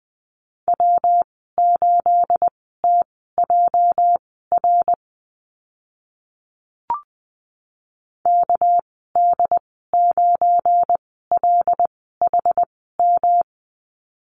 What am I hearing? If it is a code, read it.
Morse code, heard as a keyed tone, about 20 wpm, spaced as for 15 wpm, 700 Hz. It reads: W8TJR KD9LHM
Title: 245 Lesson 39g - Letter z - Call Signs 20wpm